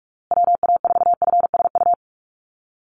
Morse code, heard as a keyed tone, about 44 wpm, 700 Hz. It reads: WA4FSU
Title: USA Call Signs Repeated 2x 45wpm